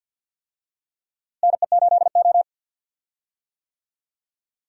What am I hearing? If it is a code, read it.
Morse code, heard as a keyed tone, about 52 wpm, 700 Hz. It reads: NE8O
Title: USA Call Signs - 4 Characters - Repeated 2x - 50wpm